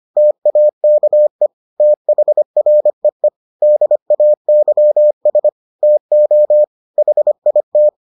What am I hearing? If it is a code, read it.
Morse code, heard as a keyed tone, about 25 wpm, 600 Hz. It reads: TAKE THREE DAYS TO HIT